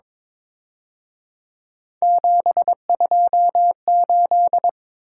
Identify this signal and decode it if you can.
Morse code, heard as a keyed tone, about 26 wpm, 700 Hz. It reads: 728
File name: Speed-Racing - Three Letter-Number Combinations - 22wpm